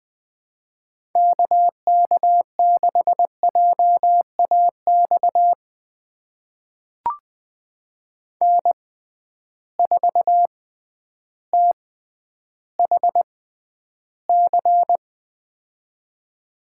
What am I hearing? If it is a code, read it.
Morse code, heard as a keyed tone, about 20 wpm, spaced as for 11 wpm, 700 Hz. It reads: KK6JAX N4THC
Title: USA Call Signs - 6x Character Spacing - 20wpm